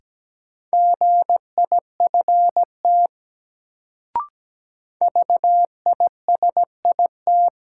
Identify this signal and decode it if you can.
Morse code, heard as a keyed tone, about 17 wpm, 700 Hz. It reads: GIFT VISIT
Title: Sentences from Top 900 Words - New Word Drill - 17wpm